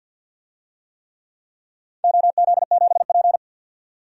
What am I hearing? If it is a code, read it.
Morse code, heard as a keyed tone, about 49 wpm, 700 Hz. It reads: OZ7P
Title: DXCC-100 Call Signs - 4 Characters - 50wpm